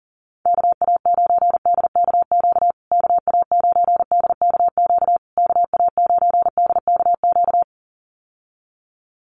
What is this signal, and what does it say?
Morse code, heard as a keyed tone, about 40 wpm, 700 Hz. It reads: KA9DKQ KA9DKQ KA9DKQ
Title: USA Call Signs Repeated 3x 40wpm